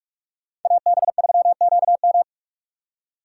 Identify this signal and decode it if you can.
Morse code, heard as a keyed tone, about 46 wpm, 700 Hz. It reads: AB3QM
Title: USA Call Signs - 5 Characters - 4x Character Spacing - 45wpm